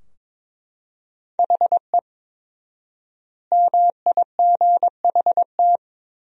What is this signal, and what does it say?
Morse code, heard as a keyed tone, about 22 wpm, 700 Hz. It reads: HE MIGHT
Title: 4x Word-Spacing - Sentences from Top 300 Words - 22wpm